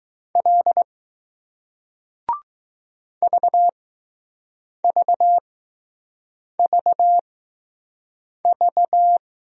Morse code, heard as a keyed tone, about 20 wpm, spaced as for 6 wpm, 700 Hz. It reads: L VVVV